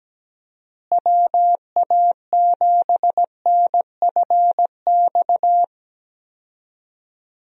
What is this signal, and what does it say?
Morse code, heard as a keyed tone, about 17 wpm, 700 Hz. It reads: WA7NFX
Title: USA Call Signs - 6 Characters - Repeated 2x - 17wpm